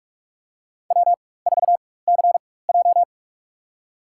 Morse code, heard as a keyed tone, about 46 wpm, spaced as for 21 wpm, 700 Hz. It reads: W4CJ
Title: USA Call Signs - 4 Characters - 4x Character Spacing - 45wpm